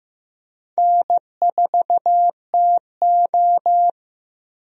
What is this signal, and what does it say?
Morse code, heard as a keyed tone, about 16 wpm, 700 Hz. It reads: N4TO
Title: USA Call Signs - 4 Characters - 15wpm